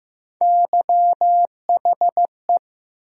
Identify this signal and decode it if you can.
Morse code, heard as a keyed tone, about 15 wpm, 700 Hz. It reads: YHE
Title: USA Call Sign Postfixes - 3 Letters 15wpm